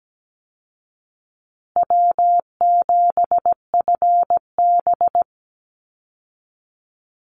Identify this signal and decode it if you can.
Morse code, heard as a keyed tone, about 17 wpm, 700 Hz. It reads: W7FB
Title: Call Signs Repeated 3x - 17wpm